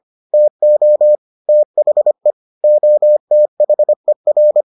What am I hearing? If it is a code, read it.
Morse code, heard as a keyed tone, about 25 wpm, 600 Hz. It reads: TO THE OTHER